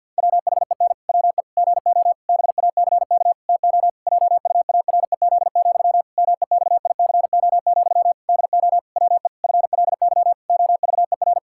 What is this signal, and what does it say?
Morse code, heard as a keyed tone, about 48 wpm, 700 Hz. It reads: WHEN WE GO BACK TO JUAREZ, MEXICO, DO WE FLY OVER
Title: Pangram Challenge v2 - 50wpm